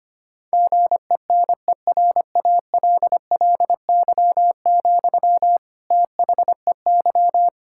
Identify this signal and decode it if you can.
Morse code, heard as a keyed tone, about 25 wpm, 700 Hz. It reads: GENERALLY, THEY
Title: Sentences from Top 1000 Words - Review Part 6 - 25wpm